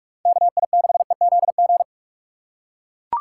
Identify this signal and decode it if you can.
Morse code, heard as a keyed tone, about 44 wpm, 700 Hz. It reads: KI6EZG
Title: USA Call Signs - 6 Characters - 6x Character Spacing - 45wpm